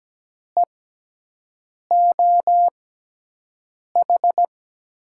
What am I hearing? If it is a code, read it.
Morse code, heard as a keyed tone, about 17 wpm, spaced as for 6 wpm, 700 Hz. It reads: EOH
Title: USA Call Signs - 6 Characters - 6x Character Spacing - 17wpm